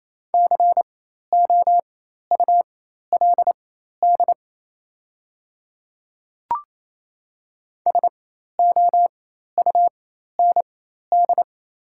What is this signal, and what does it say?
Morse code, heard as a keyed tone, about 28 wpm, spaced as for 13 wpm, 700 Hz. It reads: COULD SOUND
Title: 4x Character Spacing - Top 100 Words - 28wpm